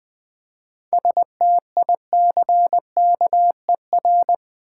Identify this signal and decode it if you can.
Morse code, heard as a keyed tone, about 20 wpm, 690 Hz. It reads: STICKER
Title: Easy Words with Hints - 20wpm